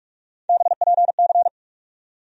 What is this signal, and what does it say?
Morse code, heard as a keyed tone, about 42 wpm, 700 Hz. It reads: BPC
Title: USA Call Sign Postfixes - 3 Letters 45wpm